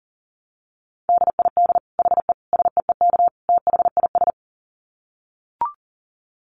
Morse code, heard as a keyed tone, about 40 wpm, 700 Hz. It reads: DID HE SEEK THIS
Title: Dit Challenge 40wpm